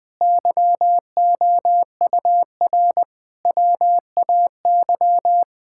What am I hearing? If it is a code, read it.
Morse code, heard as a keyed tone, about 21 wpm, 700 Hz. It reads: YOUR WAY
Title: Sets of 8 Words - 20wpm